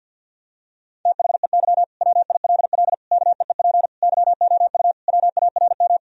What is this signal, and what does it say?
Morse code, heard as a keyed tone, about 50 wpm, 700 Hz. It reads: THEY WILL KEEP YOU WARM
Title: Sentences from Top 1000 Words - Review Part 3 - 50wpm